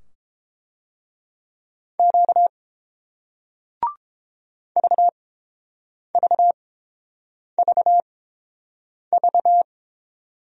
Morse code, heard as a keyed tone, about 32 wpm, spaced as for 7 wpm, 700 Hz. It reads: Q VVVV